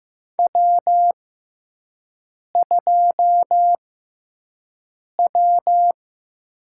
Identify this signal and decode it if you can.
Morse code, heard as a keyed tone, about 15 wpm, spaced as for 5 wpm, 700 Hz. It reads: W2W
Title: USA Call Signs - 5 Characters - 6x Character Spacing - 15wpm